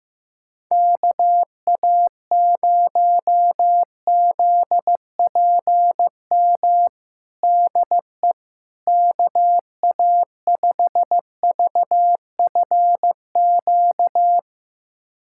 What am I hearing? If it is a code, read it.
Morse code, heard as a keyed tone, about 16 wpm, 700 Hz. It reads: KA0ZPM DE KA5VFQ
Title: USA Callsign DE Callsign - 6 Characters - 15wpm